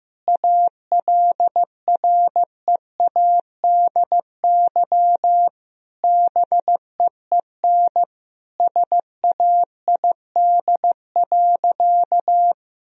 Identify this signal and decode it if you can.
Morse code, heard as a keyed tone, about 15 wpm, 700 Hz. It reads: ALREADY BEEN SAID.
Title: Sentences from Top 1000 Words - Review Part 1 - 15wpm